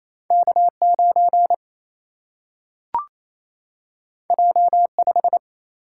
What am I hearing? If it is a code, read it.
Morse code, heard as a keyed tone, about 28 wpm, 700 Hz. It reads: K 9 J 5
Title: Single-Letter Single-Number 28wpm